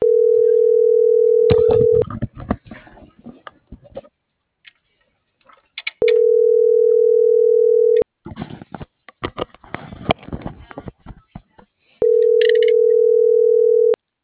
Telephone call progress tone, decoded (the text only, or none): ringback tone